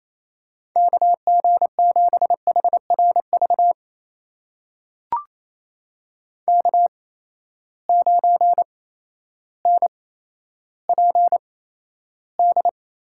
Morse code, heard as a keyed tone, about 28 wpm, spaced as for 13 wpm, 700 Hz. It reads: KG7HRV K9NPD
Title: USA Call Signs - 8x Character Spacing - 28wpm